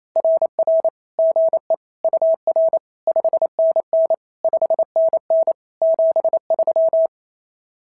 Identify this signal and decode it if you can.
Morse code, heard as a keyed tone, about 27 wpm, 590 Hz. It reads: RR GE UR 5NN 5NN 73
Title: SOTA QSO Practice 28wpm